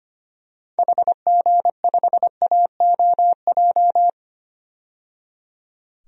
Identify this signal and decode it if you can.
Morse code, heard as a keyed tone, about 25 wpm, 690 Hz. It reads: HG5AOJ
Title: DXCC-100 Call Signs - 6 Characters - 25wpm